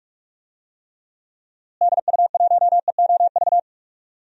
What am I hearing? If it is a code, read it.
Morse code, heard as a keyed tone, about 43 wpm, 700 Hz. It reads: DU1EOV
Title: DXCC-100 Call Signs - 6 Characters - Repeated 2x - 45wpm